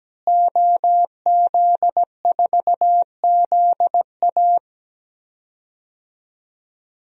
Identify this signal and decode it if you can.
Morse code, heard as a keyed tone, about 17 wpm, 700 Hz. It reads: OZ4ZA
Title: DXCC-100 Call Signs - 5 Characters - Repeated 2x - 17wpm